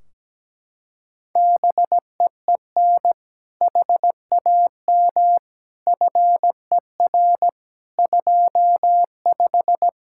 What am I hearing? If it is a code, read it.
Morse code, heard as a keyed tone, about 17 wpm, 700 Hz. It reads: BEEN HAM FER 25
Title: Practice QSO - Exchange 3 - More Info - 17wpm